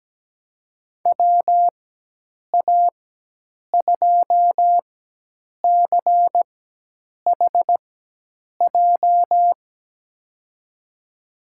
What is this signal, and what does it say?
Morse code, heard as a keyed tone, about 17 wpm, spaced as for 8 wpm, 700 Hz. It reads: WA2CHJ
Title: USA Call Signs - 6 Characters - 4x Character Spacing - 17wpm